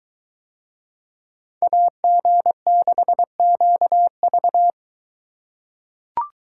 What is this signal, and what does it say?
Morse code, heard as a keyed tone, about 22 wpm, 700 Hz. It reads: AG6QV